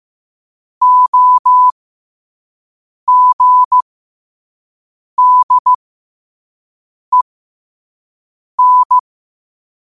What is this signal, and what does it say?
Morse code, heard as a keyed tone, about 15 wpm, spaced as for 5 wpm, 1000 Hz. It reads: OGDEN